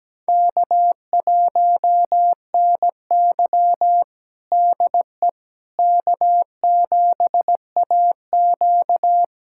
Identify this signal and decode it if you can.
Morse code, heard as a keyed tone, about 17 wpm, 700 Hz. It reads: K1NY DE K7AQ
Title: USA Callsign DE Callsign - 4 Characters - 17wpm